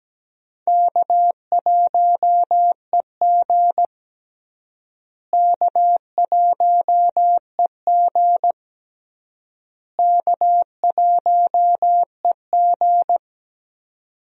Morse code, heard as a keyed tone, about 17 wpm, 700 Hz. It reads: K1EG K1EG K1EG
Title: USA Call Signs - 4 Characters - Repeated 3x - 17wpm